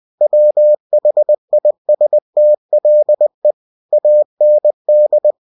Morse code, heard as a keyed tone, about 20 wpm, 600 Hz. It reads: WHISTLE AND
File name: cw-books / winnie-the-pooh / winnie-20wpm-page-0046